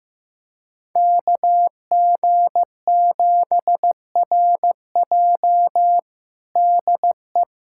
Morse code, heard as a keyed tone, about 15 wpm, 700 Hz. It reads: KG7RJ DE
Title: USA Callsign DE Callsign - 5 Characters - 15wpm